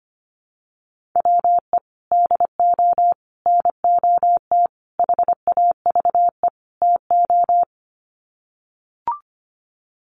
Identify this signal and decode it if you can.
Morse code, heard as a keyed tone, about 25 wpm, 700 Hz. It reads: WE DO NOT HAVE TO
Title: Sets of 5 Words 25wpm